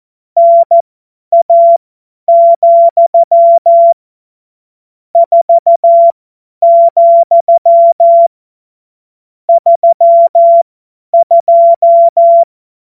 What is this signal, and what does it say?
Morse code, heard as a keyed tone, about 14 wpm, spaced as for 9 wpm, 680 Hz. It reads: NA, 4,32
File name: Morsekurs / Witze / Witz 28. Dezember 2021, 50 F70